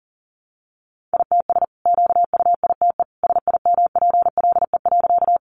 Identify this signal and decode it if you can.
Morse code, heard as a keyed tone, about 40 wpm, 700 Hz. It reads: ITS QUITE SIMPLE.
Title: Sentences from Top 700 Words 40wpm